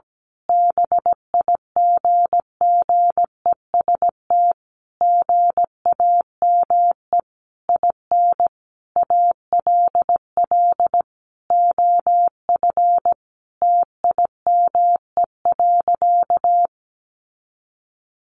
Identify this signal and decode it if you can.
Morse code, heard as a keyed tone, about 17 wpm, 700 Hz. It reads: BIGGEST GAME IN ALL OF TIME.
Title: A Sound Of Thunder 17wpm